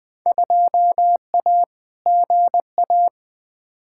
Morse code, heard as a keyed tone, about 20 wpm, 700 Hz. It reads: 2A GA
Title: Field Day Exchanges - 20wpm